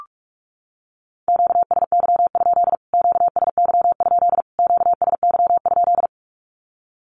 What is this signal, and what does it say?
Morse code, heard as a keyed tone, about 44 wpm, 700 Hz. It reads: QSY? QSY? QSY?
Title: Q-Codes Repeated 3x 45wpm